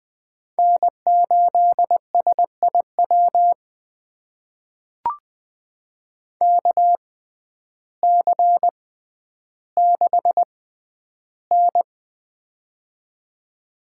Morse code, heard as a keyed tone, about 20 wpm, spaced as for 11 wpm, 700 Hz. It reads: N8SIW KC6N
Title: USA Call Signs - 6x Character Spacing - 20wpm